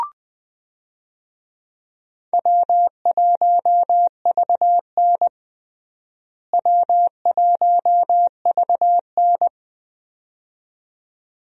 Morse code, heard as a keyed tone, about 21 wpm, 700 Hz. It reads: W1VN W1VN
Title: USA Call Signs - 4 Characters - Repeated 2x - 20wpm